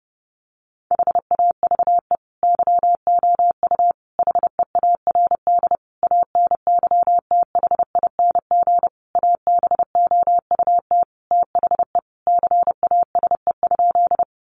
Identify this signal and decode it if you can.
Morse code, heard as a keyed tone, about 28 wpm, 700 Hz. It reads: HAVE YOU HEARD ANYTHING ABOUT THE CASE?
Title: Sentences from Top 1000 Words 30wpm